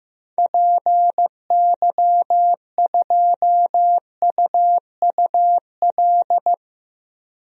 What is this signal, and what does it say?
Morse code, heard as a keyed tone, about 15 wpm, 700 Hz. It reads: PY2UUL
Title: DXCC-100 Call Signs - 6 Characters - 15wpm